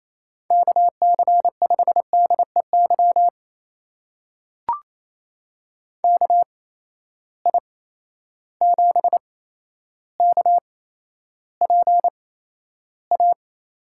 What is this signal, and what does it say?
Morse code, heard as a keyed tone, about 28 wpm, spaced as for 12 wpm, 700 Hz. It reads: KC5DEY KI7KPA
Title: USA Call Signs - 6 Characters - 8x Character Spacing - 28wpm